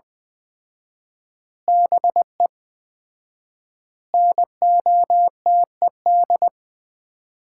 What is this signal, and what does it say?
Morse code, heard as a keyed tone, about 21 wpm, 700 Hz. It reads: BE NOTED
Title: 4x Word-Spacing - Sets of 5 Words - 20wpm